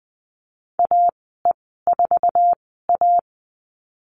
Morse code, heard as a keyed tone, about 20 wpm, spaced as for 15 wpm, 700 Hz. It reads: AE4A
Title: 048 Lesson 9g - Number 4 - Call Signs 20wpm